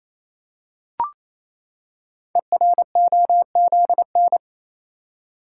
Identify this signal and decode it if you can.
Morse code, heard as a keyed tone, about 28 wpm, 700 Hz. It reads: EROZN
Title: Code Groups - 5 Letters - 28wpm